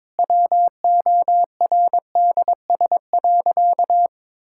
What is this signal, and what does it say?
Morse code, heard as a keyed tone, about 22 wpm, 700 Hz. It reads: WORDS.
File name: Difficult Sentences with Hints - 22wpm